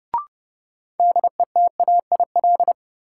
Morse code, heard as a keyed tone, about 32 wpm, 700 Hz. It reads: DETAIL
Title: Sentences from Top 1500 Words - New Word Drill - 30wpm